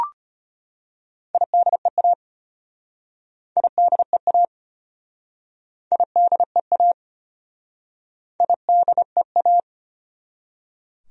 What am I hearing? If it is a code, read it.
Morse code, heard as a keyed tone, about 33 wpm, 700 Hz. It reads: IDEA IDEA IDEA IDEA